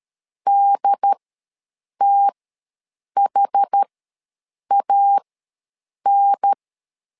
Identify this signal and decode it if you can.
Morse code, heard as a keyed tone, about 13 wpm, spaced as for 7 wpm, 800 Hz. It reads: DTHAN
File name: LEZIONI / 2 UHDN / 2 UHDN ws 12wpm 1708